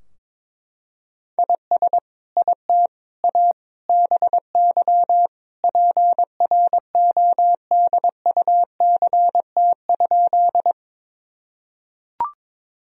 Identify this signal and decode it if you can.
Morse code, heard as a keyed tone, about 22 wpm, 700 Hz. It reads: IS IT A BY PRODUCT?
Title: Sentences from Top 1000 Words - Review Part 2 - 22wpm